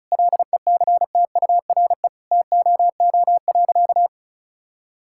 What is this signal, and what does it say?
Morse code, heard as a keyed tone, about 34 wpm, 700 Hz. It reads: LECTURE TOO.
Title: Sentences from Top 1600 Words 35wpm